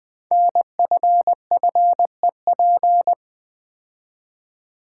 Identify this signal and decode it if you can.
Morse code, heard as a keyed tone, about 20 wpm, 700 Hz. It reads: NFFEP
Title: Code Groups - 5 Letters - 20wpm